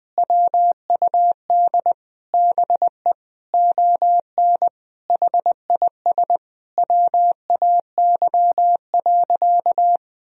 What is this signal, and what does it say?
Morse code, heard as a keyed tone, about 19 wpm, 700 Hz. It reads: WUD BE ON HIS WAY.
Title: Sentences from Top 300 Words and Abbreviations 20wpm